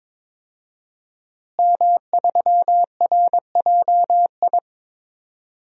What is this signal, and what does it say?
Morse code, heard as a keyed tone, about 22 wpm, 700 Hz. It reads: M3RJI
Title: DXCC-100 Call Signs - 5 Characters - Repeated 2x - 22wpm